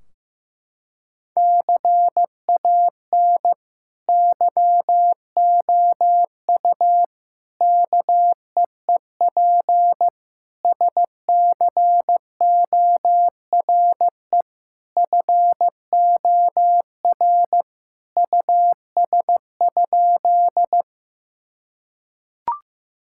Morse code, heard as a keyed tone, about 15 wpm, 700 Hz. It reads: CAN YOU KEEP SCORE FOR US?
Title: Sentences from Top 1600 Words 15wpm